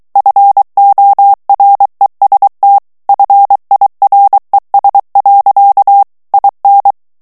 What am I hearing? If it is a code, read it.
Morse code, heard as a keyed tone, about 23 wpm, 800 Hz. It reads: FOREST FIRES. IN
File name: morse mp3 / ANGLAIS texte clair 2 vitesse 22 mots minute